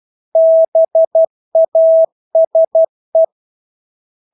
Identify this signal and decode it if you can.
Morse code, heard as a keyed tone, about 12 wpm, 650 Hz. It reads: BASE
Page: Engelske ord 12 wpm